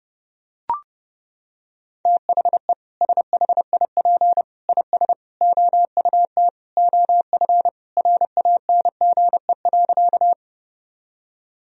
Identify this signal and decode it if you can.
Morse code, heard as a keyed tone, about 30 wpm, 700 Hz. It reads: THE SHIP IS OUT OF RANGE.
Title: Sentences from Top 1000 Words - Review Part 4 - 30wpm